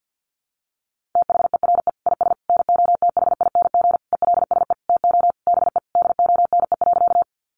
Morse code, heard as a keyed tone, about 50 wpm, 700 Hz. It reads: THERE IS NOTHING ELSE TO BE DONE.
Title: Sentences from Top 800 Words 50wpm